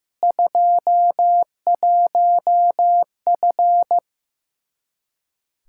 Morse code, heard as a keyed tone, about 16 wpm, 700 Hz. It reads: 21F
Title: Practice QSO - Exchange 2 - Info - 15wpm